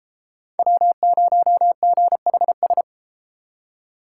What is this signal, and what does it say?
Morse code, heard as a keyed tone, about 32 wpm, 700 Hz. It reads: W0GHS